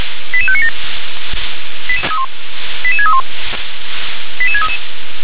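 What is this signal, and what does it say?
Five-tone selective call (ZVEI), heard as quick runs of five tones, tones of about 70 ms. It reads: 88597 88631 88633
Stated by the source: ZVEI five-tone selective call, code 88512, received by SDR.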